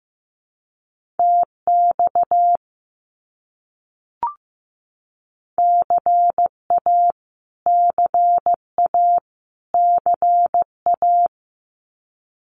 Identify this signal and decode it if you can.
Morse code, heard as a keyed tone, about 15 wpm, 700 Hz. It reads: TX CA CA CA
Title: US States Repeated 3x 15wpm